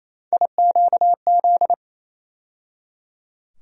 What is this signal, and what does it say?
Morse code, heard as a keyed tone, about 28 wpm, 700 Hz. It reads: IQZ
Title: USA Call Sign Postfixes - 3 Letters 28wpm